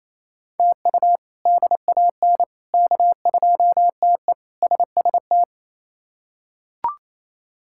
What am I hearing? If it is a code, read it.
Morse code, heard as a keyed tone, about 28 wpm, 700 Hz. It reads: TU DAN K2TE SST
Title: K1USN SST QSO Practice - 28wpm